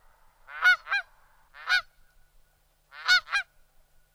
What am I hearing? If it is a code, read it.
There is no code.